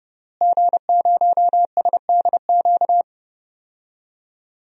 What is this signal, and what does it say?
Morse code, heard as a keyed tone, about 30 wpm, 700 Hz. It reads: G0SDQ
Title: DXCC-100 Call Signs - 5 Characters - Repeated 3x - 30wpm